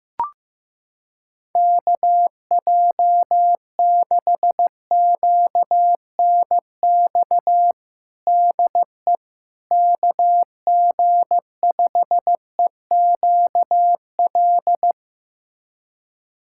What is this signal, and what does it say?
Morse code, heard as a keyed tone, about 15 wpm, 700 Hz. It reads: KJ6QNX DE KG5EQL
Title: USA Callsign DE Callsign - 6 Characters - 15wpm